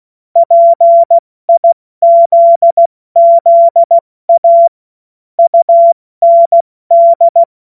Morse code, heard as a keyed tone, about 16 wpm, 680 Hz. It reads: PIZZA UND
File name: Morsekurs / Witze / Witz 27. Juni 2023 70 F80